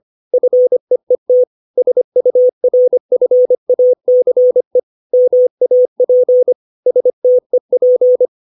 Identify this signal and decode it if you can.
Morse code, heard as a keyed tone, about 25 wpm, 500 Hz. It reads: FEET SURFACE MAP STEP
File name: 632 words / 632 Words 25WPM